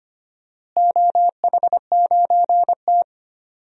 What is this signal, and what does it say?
Morse code, heard as a keyed tone, about 25 wpm, 700 Hz. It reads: OH9T
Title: DXCC-100 Call Signs - 4 Characters - Repeated 2x - 25wpm